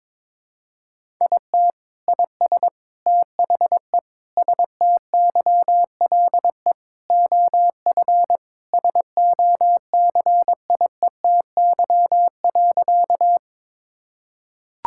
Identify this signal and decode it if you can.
Morse code, heard as a keyed tone, about 22 wpm, 700 Hz. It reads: IT IS THE STYLE OF SOCIETY.
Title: Sentences from Top 1000 Words - Review Part 5 - 22wpm